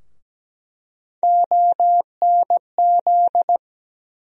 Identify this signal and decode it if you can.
Morse code, heard as a keyed tone, about 17 wpm, 700 Hz. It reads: ONZ
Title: USA Call Sign Postfixes - 3 Letters 17wpm